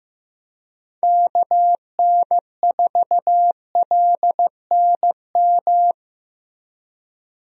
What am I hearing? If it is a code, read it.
Morse code, heard as a keyed tone, about 15 wpm, 700 Hz. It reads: KN4LNM
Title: USA Call Signs - 6 Characters - Repeated 2x - 15wpm